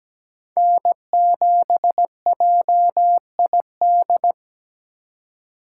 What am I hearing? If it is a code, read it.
Morse code, heard as a keyed tone, about 17 wpm, 700 Hz. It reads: N7JID
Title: USA Call Signs - 5 Characters - 2x Character Spacing - 17wpm